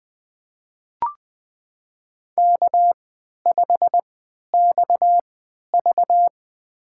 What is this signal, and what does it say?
Morse code, heard as a keyed tone, about 20 wpm, spaced as for 11 wpm, 700 Hz. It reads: K5XV
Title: USA Call Signs - 4 Characters - 3x Character Spacing - 20wpm